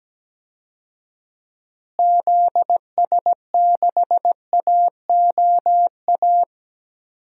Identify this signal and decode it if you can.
Morse code, heard as a keyed tone, about 17 wpm, 700 Hz. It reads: ZS6AOA
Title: DXCC-100 Call Signs - 6 Characters - Repeated 3x - 17wpm